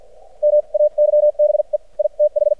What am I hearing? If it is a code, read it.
Morse code, heard as a keyed tone, about 56 wpm, spaced as for 44 wpm, 600 Hz. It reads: MAYBE I TH
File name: archived / CRSnet